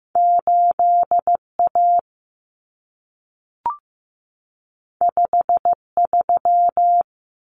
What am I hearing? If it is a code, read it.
Morse code, heard as a keyed tone, about 15 wpm, 700 Hz. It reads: 8A 53
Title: Common Cut Numbers - 2-Digit 15wpm